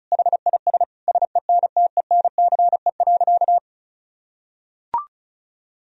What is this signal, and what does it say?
Morse code, heard as a keyed tone, about 36 wpm, 700 Hz. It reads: HIS SENTENCE.
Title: Sentences from Top 1000 Words - Review Part 1 - 35wpm